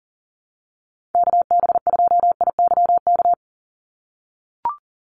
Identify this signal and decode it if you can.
Morse code, heard as a keyed tone, about 40 wpm, 700 Hz. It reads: KB2IYK
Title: USA Call Signs 40wpm